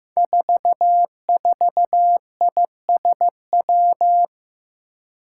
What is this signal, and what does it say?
Morse code, heard as a keyed tone, about 15 wpm, 700 Hz. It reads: 44ISW
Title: Five Challenging Letter-Number Combinations 15wpm